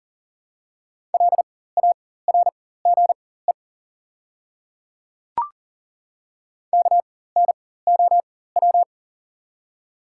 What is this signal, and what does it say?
Morse code, heard as a keyed tone, about 40 wpm, spaced as for 19 wpm, 700 Hz. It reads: LARGE KNOW
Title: 4x Character Spacing - Top 300 Words - 40wpm